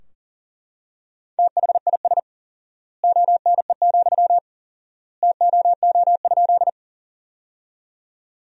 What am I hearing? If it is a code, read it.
Morse code, heard as a keyed tone, about 40 wpm, 700 Hz. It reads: THIS ONE, TOO?
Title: 4x Word-Spacing - Sentences from Top 200 Words - 40wpm